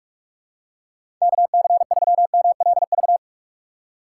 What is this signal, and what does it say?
Morse code, heard as a keyed tone, about 46 wpm, 700 Hz. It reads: KC3MLV
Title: USA Call Signs - 4x Character Spacing - 45wpm